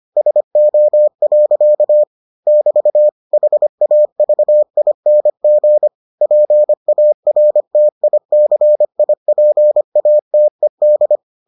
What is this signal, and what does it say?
Morse code, heard as a keyed tone, about 25 wpm, 600 Hz. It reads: SO. <BT> HAVING PARTICIPATED